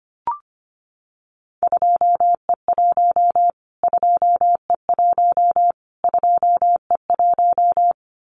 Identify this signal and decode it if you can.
Morse code, heard as a keyed tone, about 25 wpm, 700 Hz. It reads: 2E1 2E1 2E1
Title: Callsign Prefixes for Top 25 DXCC - Repeated 3x - 25wpm